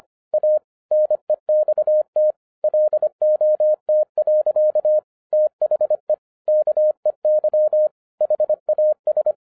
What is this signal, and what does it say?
Morse code, heard as a keyed tone, about 25 wpm, 610 Hz. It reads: A NEXT LOT. THE KEY HAS